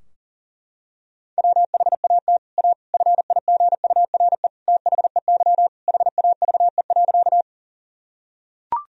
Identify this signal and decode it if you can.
Morse code, heard as a keyed tone, about 40 wpm, 700 Hz. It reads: WHAT A FIGURE THEY HAVE.
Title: Sentences from Top 1000 Words - Review Part 2 - 40wpm